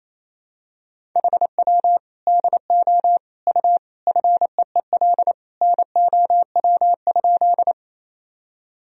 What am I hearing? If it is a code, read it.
Morse code, heard as a keyed tone, about 28 wpm, 700 Hz. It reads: HW DO U FEEL NOW?
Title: Sentences from Top 400 Words and Abbreviations 28wpm